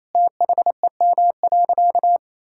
Morse code, heard as a keyed tone, about 28 wpm, 700 Hz. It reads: THEM.
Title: Sentences from Top 1000 Words - Review Part 6 - 28wpm